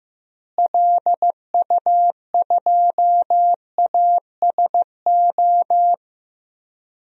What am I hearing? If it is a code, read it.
Morse code, heard as a keyed tone, about 15 wpm, 700 Hz. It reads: LU2ASO
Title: DXCC-100 Call Signs - 6 Characters - Repeated 3x - 15wpm